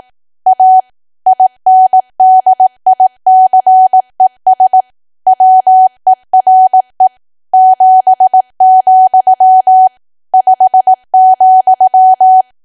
Morse code, heard as a keyed tone, about 18 wpm, 750 Hz. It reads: A INDICES WERE 7, 5,
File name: cwa / W1AW-Level2 / 18WPM